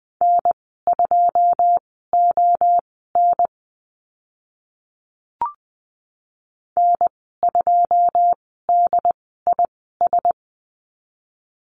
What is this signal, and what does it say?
Morse code, heard as a keyed tone, about 20 wpm, spaced as for 14 wpm, 700 Hz. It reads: N2ON N2DIS
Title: 088 Lesson 15g - Number 5 - Call Signs 20wpm